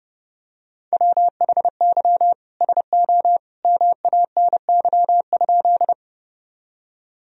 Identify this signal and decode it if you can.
Morse code, heard as a keyed tone, about 30 wpm, 700 Hz. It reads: WHY SO MANY?
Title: Sentences from Top 1000 Words - Review Part 1 - 30wpm